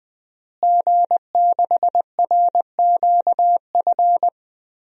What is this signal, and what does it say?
Morse code, heard as a keyed tone, about 21 wpm, 700 Hz. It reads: G6RQF
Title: DXCC-100 Call Signs - 5 Characters - Repeated 3x - 20wpm